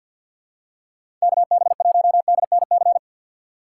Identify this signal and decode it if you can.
Morse code, heard as a keyed tone, about 51 wpm, 700 Hz. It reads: KB1DNC
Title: USA Call Signs - 6x Character Spacing - 50wpm